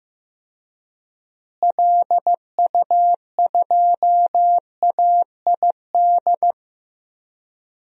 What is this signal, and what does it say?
Morse code, heard as a keyed tone, about 15 wpm, 700 Hz. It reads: LU2AID
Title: DXCC-100 Call Signs - 6 Characters - Repeated 3x - 15wpm